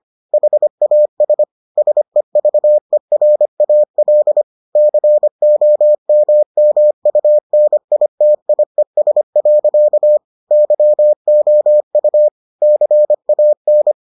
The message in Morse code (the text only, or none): HAS SEVERAL COMMUNITIES. YOU CAN